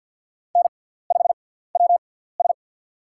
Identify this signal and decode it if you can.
Morse code, heard as a keyed tone, about 50 wpm, spaced as for 17 wpm, 700 Hz. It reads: N5WS
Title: USA Call Signs - 6x Character Spacing - 50wpm